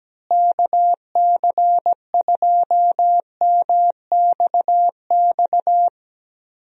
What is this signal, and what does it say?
Morse code, heard as a keyed tone, about 17 wpm, 700 Hz. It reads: KC2MXX
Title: USA Call Signs - 6 Characters - Repeated 3x - 17wpm